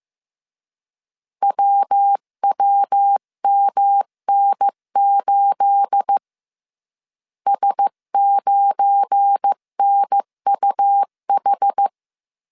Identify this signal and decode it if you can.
Morse code, heard as a keyed tone, about 15 wpm, 800 Hz. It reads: WWMN8 S9NUH
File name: CW / DET / 1 9 1500C S B 14wpm 3302